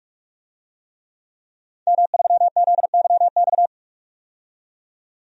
Morse code, heard as a keyed tone, about 45 wpm, 700 Hz. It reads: M3ZYX
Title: DXCC-100 Call Signs - 5 Characters - 45wpm